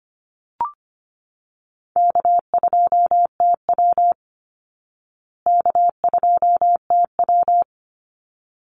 Morse code, heard as a keyed tone, about 25 wpm, 700 Hz. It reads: K2TW K2TW
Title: Easy Call Signs Repeated 2x - 25wpm